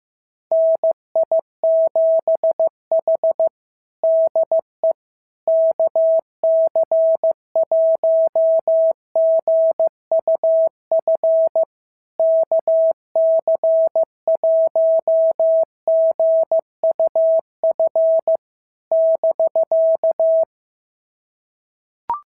Morse code, heard as a keyed tone, about 14 wpm, 650 Hz. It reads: NI7H DE KC1GUF KC1GUF <BK>
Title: SOTA QSO Practice 15wpm